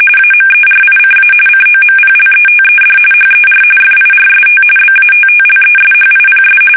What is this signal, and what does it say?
RTTY, about 45.45 baud. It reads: AMERICA COMPLETED HER FIRST BATTLESHIPS
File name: fsk850 / battleshipnj